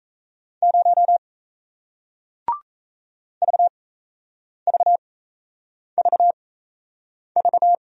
Morse code, heard as a keyed tone, about 38 wpm, spaced as for 8 wpm, 700 Hz. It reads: 0 VVVV